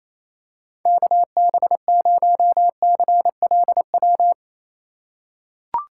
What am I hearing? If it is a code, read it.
Morse code, heard as a keyed tone, about 28 wpm, 700 Hz. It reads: KB0CLW
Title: USA Call Signs - 2x Character Spacing - 28wpm